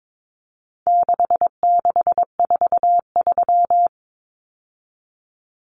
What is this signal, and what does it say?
Morse code, heard as a keyed tone, about 22 wpm, 700 Hz. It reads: 6643
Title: Common Cut Numbers - 4-Digit 22wpm